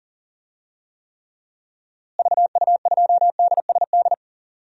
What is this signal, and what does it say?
Morse code, heard as a keyed tone, about 40 wpm, 700 Hz. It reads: VU2DSD
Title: DXCC-100 Call Signs - 6 Characters - Repeated 2x - 40wpm